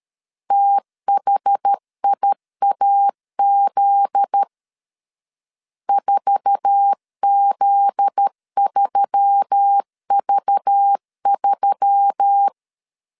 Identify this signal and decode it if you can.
Morse code, heard as a keyed tone, about 13 wpm, 800 Hz. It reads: THIAZ 4Z3V3
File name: LEZIONI / 9 2 12wpm 2301